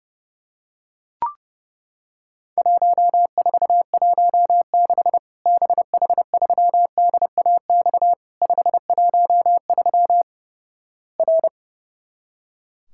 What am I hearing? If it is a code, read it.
Morse code, heard as a keyed tone, about 31 wpm, 700 Hz. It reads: 1416 BH3DAX 513 R
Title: Worked All Europe DX Contest - QTC Exchange - 30wpm